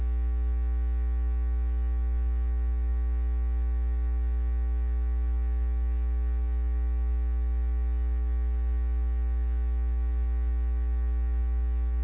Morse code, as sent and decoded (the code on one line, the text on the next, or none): none